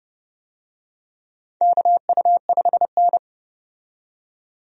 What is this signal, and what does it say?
Morse code, heard as a keyed tone, about 32 wpm, 700 Hz. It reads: KU5N
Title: USA Call Signs - 4 Characters - 30wpm